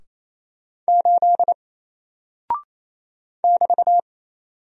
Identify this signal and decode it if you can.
Morse code, heard as a keyed tone, about 27 wpm, 700 Hz. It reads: 8 <BT>
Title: Single Character (Letter:Number:Punctuation:Prosign) Alt 28wpm